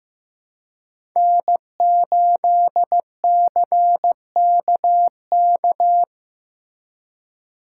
Morse code, heard as a keyed tone, about 17 wpm, 700 Hz. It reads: N8CKK
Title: Speed Racing - USA Call Signs 15wpm